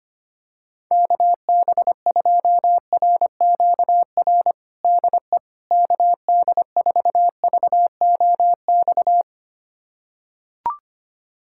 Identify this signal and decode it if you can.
Morse code, heard as a keyed tone, about 25 wpm, 700 Hz. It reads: KB2RQR DE KD4VOX
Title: USA Callsign DE Callsign - 6 Characters - 25wpm